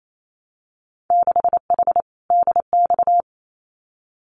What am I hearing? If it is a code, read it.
Morse code, heard as a keyed tone, about 28 wpm, 700 Hz. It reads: 6H DX
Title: Winter Field Day Exchanges Encore 28wpm